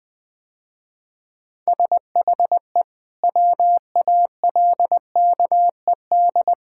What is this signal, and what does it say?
Morse code, heard as a keyed tone, about 20 wpm, 700 Hz. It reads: SHE WALKED
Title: Sentences with 3-Word Phrases - Spoken Beginning - 20wpm